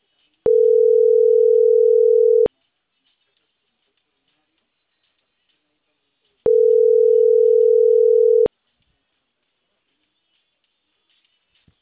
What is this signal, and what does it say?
Ringback tone (North America), 440 + 480 Hz.